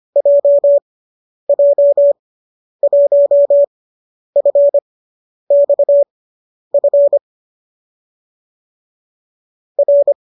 Morse code, heard as a keyed tone, about 25 wpm, spaced as for 10 wpm, 570 Hz. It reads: JJ1FXF R
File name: qso005 10